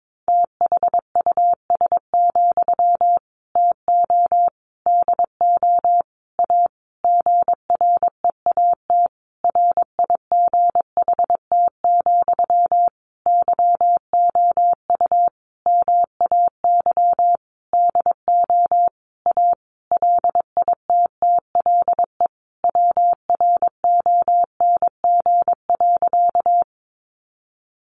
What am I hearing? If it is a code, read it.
Morse code, heard as a keyed tone, about 22 wpm, 700 Hz. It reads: THUS, TO DO A GREAT RIGHT, YOU MAY DO A LITTLE WRONG.
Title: Sentences from Top 900 Words 22wpm